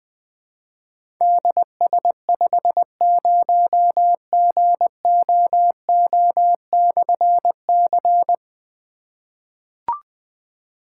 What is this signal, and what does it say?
Morse code, heard as a keyed tone, about 20 wpm, 700 Hz. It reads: DS50GOO/C
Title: Difficult Call Signs 20wpm